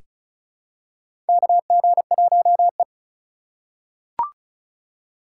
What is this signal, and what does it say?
Morse code, heard as a keyed tone, about 34 wpm, 700 Hz. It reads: KG1E
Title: USA Call Signs - 4 Characters - 2x Character Spacing - 35wpm